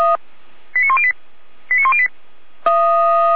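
Five-tone selective call (ZVEI), heard as quick runs of five tones, tones of about 70 ms. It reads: 89108 89108
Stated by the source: Five-tone ZVEI selective call received by SDR.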